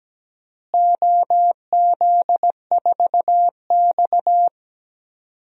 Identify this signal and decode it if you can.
Morse code, heard as a keyed tone, about 17 wpm, 700 Hz. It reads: OZ4X
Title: DXCC-100 Call Signs - 4 Characters - Repeated 3x - 17wpm